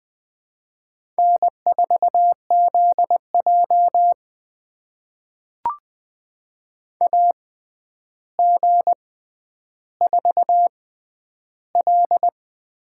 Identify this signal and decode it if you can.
Morse code, heard as a keyed tone, about 21 wpm, spaced as for 10 wpm, 700 Hz. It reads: N4ZJ AG4L
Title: USA Call Signs - 4 Characters - 6x Character Spacing - 20wpm